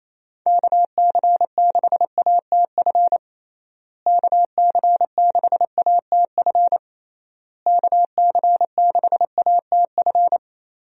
Morse code, heard as a keyed tone, about 27 wpm, 700 Hz. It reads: KC6ATF KC6ATF KC6ATF
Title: USA Call Signs - 6 Characters - Repeated 3x - 28wpm